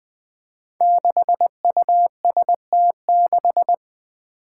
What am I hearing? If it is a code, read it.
Morse code, heard as a keyed tone, about 21 wpm, 700 Hz. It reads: 6UST6
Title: Five Challenging Letter-Number Combinations 20wpm